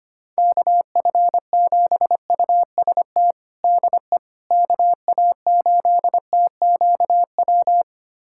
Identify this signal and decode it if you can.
Morse code, heard as a keyed tone, about 25 wpm, 700 Hz. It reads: KF7UST DE KA8TQW
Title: USA Callsign DE Callsign 25wpm